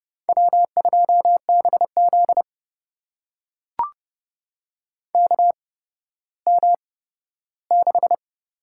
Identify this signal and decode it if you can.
Morse code, heard as a keyed tone, about 30 wpm, spaced as for 15 wpm, 700 Hz. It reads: W2BZ KM6
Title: USA Call Signs - 4 Characters - 8x Character Spacing - 30wpm